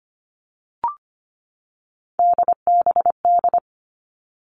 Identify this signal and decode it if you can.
Morse code, heard as a keyed tone, about 25 wpm, 700 Hz. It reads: DBD
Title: All Cut Numbers - 3-Digit 25wpm